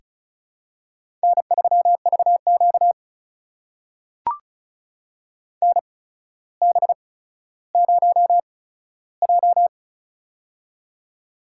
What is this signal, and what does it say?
Morse code, heard as a keyed tone, about 36 wpm, spaced as for 15 wpm, 700 Hz. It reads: N3VQ NB0J
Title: USA Call Signs - 4 Characters - 8x Character Spacing - 35wpm